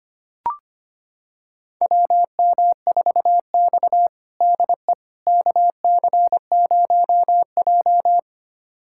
Morse code, heard as a keyed tone, about 25 wpm, 700 Hz. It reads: WM4X DE KC0J
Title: USA Callsign DE Callsign - 4 Characters - 25wpm